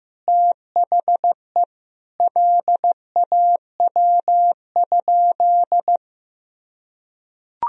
Morse code, heard as a keyed tone, about 16 wpm, 700 Hz. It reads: THE LAW?
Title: Sentences from Top 1000 Words - Review Part 4 - 15wpm